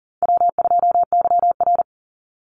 Morse code, heard as a keyed tone, about 40 wpm, 700 Hz. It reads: W2YR
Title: Easy Call Signs Repeated 3x - 40wpm